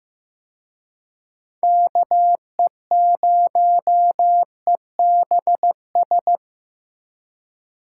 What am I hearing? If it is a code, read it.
Morse code, heard as a keyed tone, about 18 wpm, 700 Hz. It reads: KE0EBS
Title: Speed Racing - USA 2x3 Call Signs 15wpm